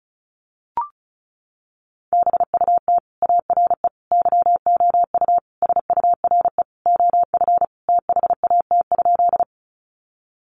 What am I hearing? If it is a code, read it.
Morse code, heard as a keyed tone, about 34 wpm, 700 Hz. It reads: BUT ARE YOU SURE OF THAT?
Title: Sentences from Top 300 Words 35wpm